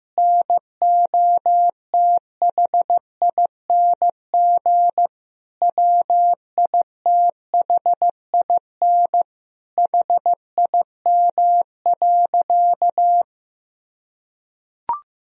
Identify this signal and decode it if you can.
Morse code, heard as a keyed tone, about 15 wpm, 700 Hz. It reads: NOTHING WITHIN HIM.
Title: Sentences from Top 1000 Words - Review Part 4 - 15wpm